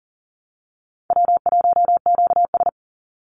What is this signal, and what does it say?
Morse code, heard as a keyed tone, about 40 wpm, 700 Hz. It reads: W1QS
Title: Call Signs Repeated 3x - 40wpm